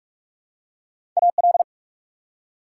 Morse code, heard as a keyed tone, about 44 wpm, 690 Hz. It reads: AL
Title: Prefixes and Suffixes 45wpm